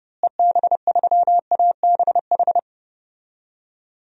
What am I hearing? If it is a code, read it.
Morse code, heard as a keyed tone, about 32 wpm, 700 Hz. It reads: EB3ABH
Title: DXCC-100 Call Signs - 6 Characters - Repeated 3x - 30wpm